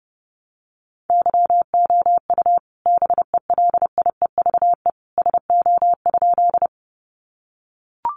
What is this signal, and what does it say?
Morse code, heard as a keyed tone, about 30 wpm, 700 Hz. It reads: YOU BELIEVE SO?
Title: Sentences from Top 1000 Words 30wpm